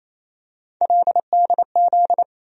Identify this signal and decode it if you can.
Morse code, heard as a keyed tone, about 28 wpm, 700 Hz. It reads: LDZ
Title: USA Call Sign Postfixes - 3 Letters 28wpm